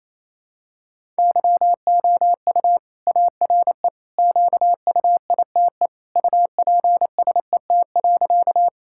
Morse code, heard as a keyed tone, about 28 wpm, 700 Hz. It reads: YOU ARE QUITE UPSET.
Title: Sentences from Top 1000 Words - Review Part 5 - 28wpm